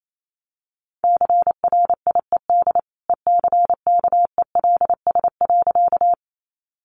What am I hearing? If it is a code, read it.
Morse code, heard as a keyed tone, about 28 wpm, 700 Hz. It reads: CRIED ECKELS.
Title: A Sound Of Thunder 28wpm